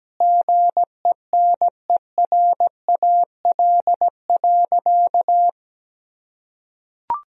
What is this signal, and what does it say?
Morse code, heard as a keyed tone, about 17 wpm, 700 Hz. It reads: GENERAL.
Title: Sentences from Top 1000 Words - Review Part 3 - 17wpm